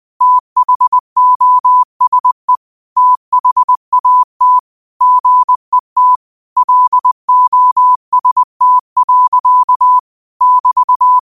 Morse code, heard as a keyed tone, about 20 wpm, 1000 Hz. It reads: THOSE THAT GET LOST. <BT>
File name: qotdfiles / 20wpm